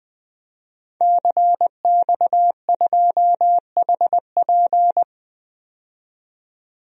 Morse code, heard as a keyed tone, about 20 wpm, 700 Hz. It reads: CX2HP
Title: DXCC-100 Call Signs - 5 Characters - 20wpm